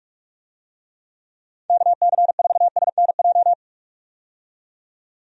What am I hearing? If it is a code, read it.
Morse code, heard as a keyed tone, about 46 wpm, 700 Hz. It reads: KC4SNJ
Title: USA Call Signs - 6 Characters - 45wpm